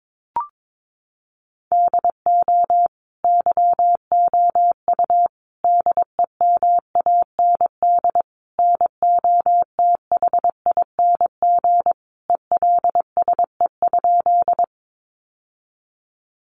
Morse code, heard as a keyed tone, about 22 wpm, 700 Hz. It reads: DO YOU DEMAND NOTHING ELSE?
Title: Sentences from Top 800 Words 22wpm